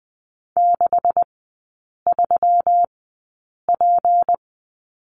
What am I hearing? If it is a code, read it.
Morse code, heard as a keyed tone, about 20 wpm, spaced as for 8 wpm, 700 Hz. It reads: 63P
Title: 152 Lesson 25e - Letter p - Three Characters 20wpm